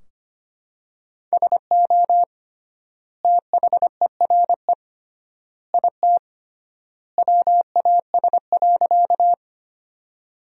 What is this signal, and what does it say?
Morse code, heard as a keyed tone, about 25 wpm, 700 Hz. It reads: SO THERE IT WAS.
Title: 3x Word-Spacing - Sentences from Top 100 Words - 25wpm